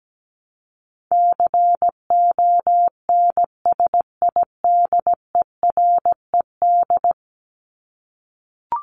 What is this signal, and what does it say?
Morse code, heard as a keyed tone, about 17 wpm, 700 Hz. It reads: CONSIDERED
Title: ED Suffix - Top 100 Words with Base Words 17wpm